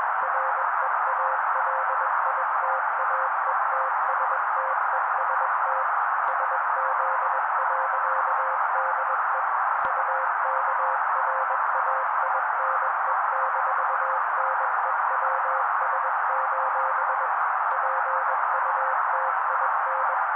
Morse code, heard as a keyed tone, about 21 wpm, 530 Hz. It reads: REALITAETSTEST SZ.DE UKRAINE-NEWS: PUTIN